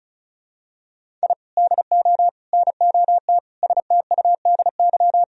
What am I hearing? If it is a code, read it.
Morse code, heard as a keyed tone, about 36 wpm, 700 Hz. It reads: I DO NOT STUDY
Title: Sentences from Top 1000 Words - Review Part 1 - 35wpm